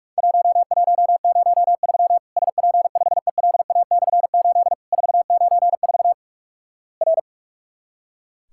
Morse code, heard as a keyed tone, about 46 wpm, 700 Hz. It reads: 1103 SP5ELA/8 494 R
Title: Worked All Europe DX Contest - QTC Exchange - 45wpm